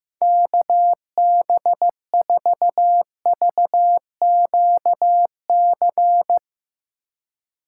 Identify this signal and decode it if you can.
Morse code, heard as a keyed tone, about 15 wpm, 700 Hz. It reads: KB4VQC
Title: Speed Racing - USA 2x3 Call Signs 15wpm